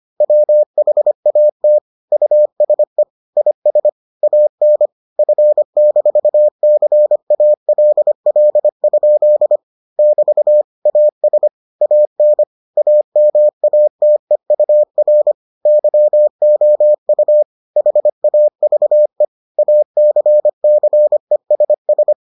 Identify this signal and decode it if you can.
Morse code, heard as a keyed tone, about 25 wpm, 600 Hz. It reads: WHAT USE IS AN F-CALL? <BT> AS AN AMATEUR YOU HAVE ACCESS